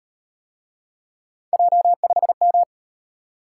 Morse code, heard as a keyed tone, about 38 wpm, 700 Hz. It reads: J5M